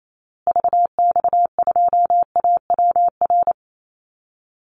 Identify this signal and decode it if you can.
Morse code, heard as a keyed tone, about 28 wpm, 700 Hz. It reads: VX2AWR
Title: Call Signs Repeated 2x - 28wpm